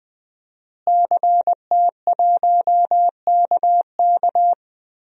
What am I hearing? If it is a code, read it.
Morse code, heard as a keyed tone, about 19 wpm, 700 Hz. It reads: CT1KK
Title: DXCC-100 Call Signs - 5 Characters - Repeated 3x - 20wpm